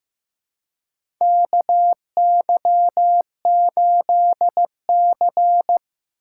Morse code, heard as a keyed tone, about 15 wpm, 700 Hz. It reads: KY8C
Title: USA Call Signs - 4x Character Spacing - 15wpm